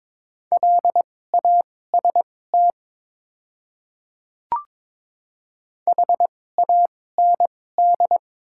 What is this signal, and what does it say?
Morse code, heard as a keyed tone, about 21 wpm, spaced as for 16 wpm, 700 Hz. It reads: LAST HAND
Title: 2x Character Spacing - Top 300 Words - 22wpm